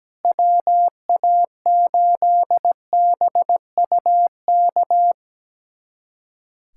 Morse code, heard as a keyed tone, about 17 wpm, 700 Hz. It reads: WA8BUK
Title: USA Callsign DE Callsign - 6 Characters - 17wpm